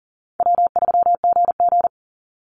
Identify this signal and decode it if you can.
Morse code, heard as a keyed tone, about 40 wpm, 700 Hz. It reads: W3GG
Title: Easy Call Signs Repeated 3x - 40wpm